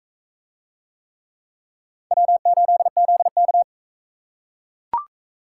Morse code, heard as a keyed tone, about 39 wpm, 700 Hz. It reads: W8ZK